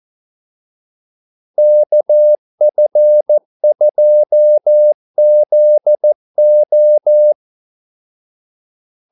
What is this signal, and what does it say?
Morse code, heard as a keyed tone, about 14 wpm, 600 Hz. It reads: KF2ZO
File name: Callsigns-14wpm